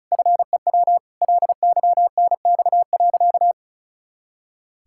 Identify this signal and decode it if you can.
Morse code, heard as a keyed tone, about 36 wpm, 700 Hz. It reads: FEW LYNX.
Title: Pangram Challenge v2 - 35wpm